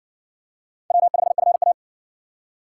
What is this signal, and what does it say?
Morse code, heard as a keyed tone, about 58 wpm, 700 Hz. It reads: W5FA